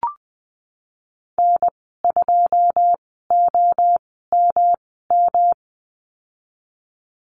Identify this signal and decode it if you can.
Morse code, heard as a keyed tone, about 20 wpm, spaced as for 15 wpm, 700 Hz. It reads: N2OMM
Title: 128 Lesson 21g - Number 6 - Call Signs 20wpm